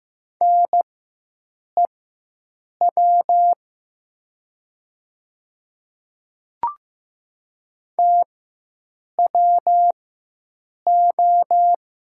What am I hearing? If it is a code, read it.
Morse code, heard as a keyed tone, about 15 wpm, spaced as for 7 wpm, 700 Hz. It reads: NEW TWO
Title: 4x Character Spacing - Top 300 Words - 15wpm